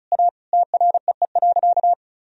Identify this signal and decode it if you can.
Morse code, heard as a keyed tone, about 36 wpm, 700 Hz. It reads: A TREE.
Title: Sentences from Top 1000 Words - Review Part 1 - 35wpm